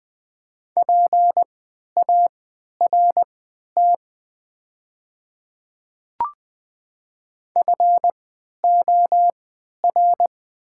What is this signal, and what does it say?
Morse code, heard as a keyed tone, about 20 wpm, spaced as for 11 wpm, 700 Hz. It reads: PART FOR
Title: 3x Character Spacing - Top 500 Words - 20wpm